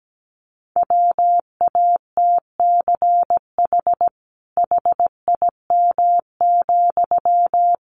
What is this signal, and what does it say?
Morse code, heard as a keyed tone, about 17 wpm, 700 Hz. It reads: WATCH HIM,
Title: Sentences from Top 300 Words 17wpm